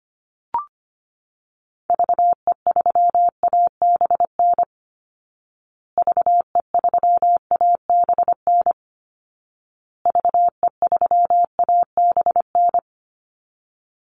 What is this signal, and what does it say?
Morse code, heard as a keyed tone, about 25 wpm, 700 Hz. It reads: VE3ABN VE3ABN VE3ABN
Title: Call Signs Repeated 2x - 25wpm